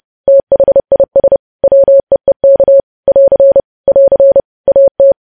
Morse code, heard as a keyed tone, about 30 wpm, 550 Hz. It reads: THIS WEEK <AR> <AR> AT
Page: news headlines in morse code at 30 wpm